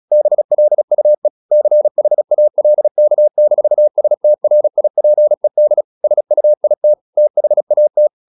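Morse code, heard as a keyed tone, about 36 wpm, 600 Hz. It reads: BLUE CHALK-STRIPED SUIT THAT
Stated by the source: Morse practice at 36 wpm, text of Ken Follett's novel Never, from CW med Gnister.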